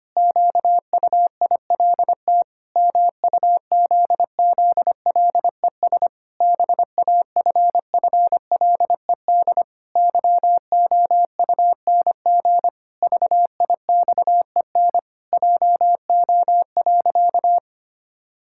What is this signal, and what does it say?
Morse code, heard as a keyed tone, about 25 wpm, 700 Hz. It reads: QUILT MUZZLES BAFFLED YOUNG VIXEN JO.
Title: Pangram Challenge v2 - 25wpm